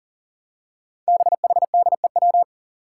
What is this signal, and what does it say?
Morse code, heard as a keyed tone, about 40 wpm, 700 Hz. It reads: BHDEW
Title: Five Challenging Letter-Number Combinations 40wpm